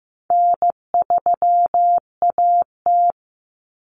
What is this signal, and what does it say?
Morse code, heard as a keyed tone, about 15 wpm, 700 Hz. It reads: N3AT
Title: Common Cut Numbers - 4-Digit 15wpm